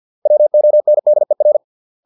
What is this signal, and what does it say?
Morse code, heard as a keyed tone, about 50 wpm, 600 Hz. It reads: WONDER